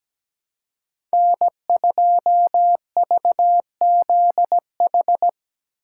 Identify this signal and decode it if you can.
Morse code, heard as a keyed tone, about 17 wpm, 700 Hz. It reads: N2VZH
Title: USA Call Signs - 5 Characters - 3x Character Spacing - 17wpm